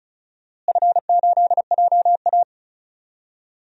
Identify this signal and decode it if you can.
Morse code, heard as a keyed tone, about 36 wpm, 700 Hz. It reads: F8JA
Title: DXCC-100 Call Signs - 4 Characters - Repeated 3x - 35wpm